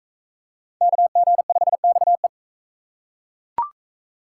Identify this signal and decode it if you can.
Morse code, heard as a keyed tone, about 40 wpm, 700 Hz. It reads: KG5XE